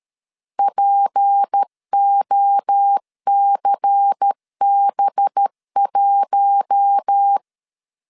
Morse code, heard as a keyed tone, about 13 wpm, 800 Hz. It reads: POCB1
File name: LEZIONI / 6 PKX0 / 6 P 12wpm 2033